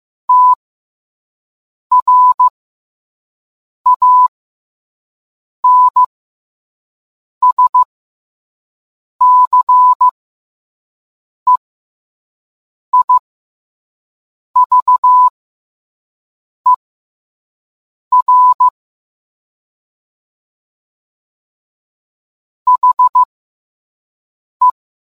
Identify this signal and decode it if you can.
Morse code, heard as a keyed tone, about 15 wpm, spaced as for 5 wpm, 1000 Hz. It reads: TRANSCEIVER HE